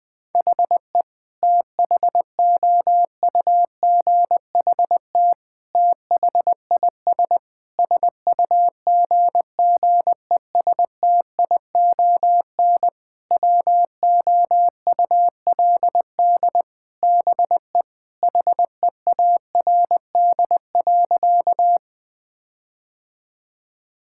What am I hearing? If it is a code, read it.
Morse code, heard as a keyed tone, about 20 wpm, 700 Hz. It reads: HE THOUGHT THIS SUGGESTION WOULD BE HEARD.
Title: Sentences from Top 1600 Words 20wpm